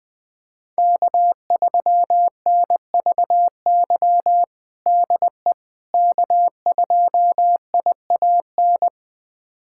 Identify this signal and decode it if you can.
Morse code, heard as a keyed tone, about 20 wpm, 700 Hz. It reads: K3NVY DE K2IAN
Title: USA Callsign DE Callsign - 5 Characters - 20wpm